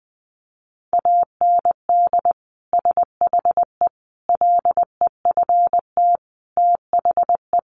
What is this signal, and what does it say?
Morse code, heard as a keyed tone, about 19 wpm, 700 Hz. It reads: AND SHE LEFT THE
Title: Sentences from Top 300 Words 20wpm